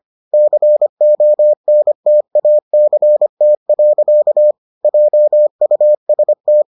Morse code, heard as a keyed tone, about 25 wpm, 600 Hz. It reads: CONTACT. JUST